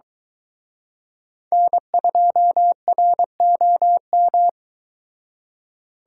Morse code, heard as a keyed tone, about 22 wpm, 700 Hz. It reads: N2ROM